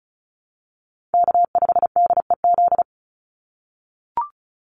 Morse code, heard as a keyed tone, about 34 wpm, 700 Hz. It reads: K5DEZ
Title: NA SOTA Chaser Callsigns 35wpm